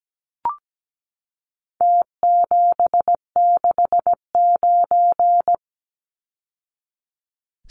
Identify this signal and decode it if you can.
Morse code, heard as a keyed tone, about 17 wpm, 700 Hz. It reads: T769
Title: Numbers - Contest Style - 4-Digit 17wpm